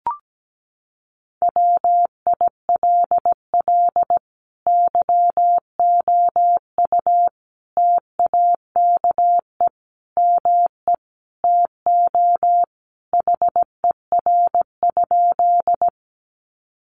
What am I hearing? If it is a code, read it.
Morse code, heard as a keyed tone, about 17 wpm, 700 Hz. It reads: WILL YOU TAKE ME TO HER?
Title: Sentences from Top 200 Words 17wpm